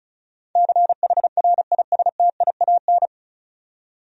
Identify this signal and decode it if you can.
Morse code, heard as a keyed tone, about 34 wpm, 700 Hz. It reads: CHRISTIAN
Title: USA Common Names - 35wpm